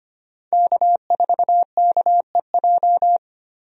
Morse code, heard as a keyed tone, about 25 wpm, 700 Hz. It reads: K4KEJ
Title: USA Call Signs - 5 Characters - 25wpm